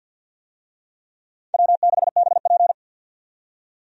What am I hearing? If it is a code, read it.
Morse code, heard as a keyed tone, about 51 wpm, 700 Hz. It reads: W6BP
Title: USA Call Signs - 4 Characters - 50wpm